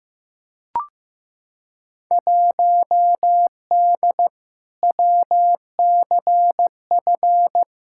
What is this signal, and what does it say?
Morse code, heard as a keyed tone, about 15 wpm, 700 Hz. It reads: 1D WCF
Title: Field Day Exchanges - 15wpm